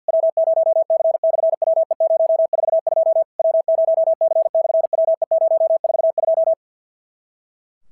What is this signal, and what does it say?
Morse code, heard as a keyed tone, about 50 wpm, 650 Hz. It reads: W0C/WE042 W0C/WE042
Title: SOTA QSO Practice 50wpm